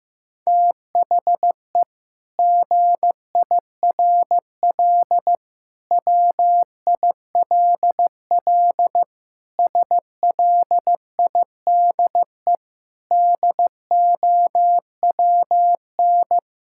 Morse code, heard as a keyed tone, about 15 wpm, 700 Hz. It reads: THE GIRL WILL SLIDE DOWN
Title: Sentences from Top 1600 Words 15wpm